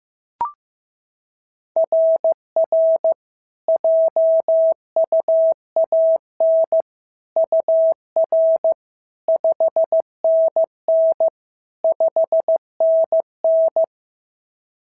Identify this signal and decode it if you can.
Morse code, heard as a keyed tone, about 14 wpm, 650 Hz. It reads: RR JUAN UR 5NN 5NN
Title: SOTA QSO Practice 15wpm